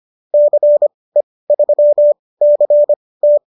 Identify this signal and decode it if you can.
Morse code, heard as a keyed tone, about 25 wpm, spaced as for 18 wpm, 590 Hz. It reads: CE3CT
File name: qso217 18